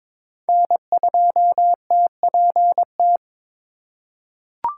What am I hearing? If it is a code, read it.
Morse code, heard as a keyed tone, about 23 wpm, 700 Hz. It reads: N2TPT
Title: USA Call Signs - 4x Character Spacing - 22wpm